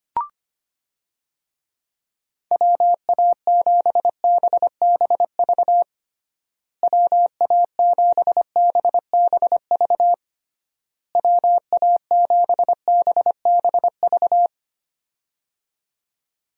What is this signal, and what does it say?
Morse code, heard as a keyed tone, about 25 wpm, 700 Hz. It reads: WA7BBV WA7BBV WA7BBV
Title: USA Call Signs - 6 Characters - Repeated 3x - 25wpm